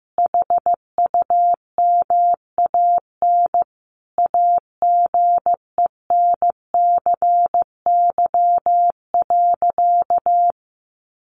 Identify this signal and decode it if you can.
Morse code, heard as a keyed tone, about 15 wpm, 700 Hz. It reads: HUMAN AGENCY.
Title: Sentences from Top 1300 Words 15wpm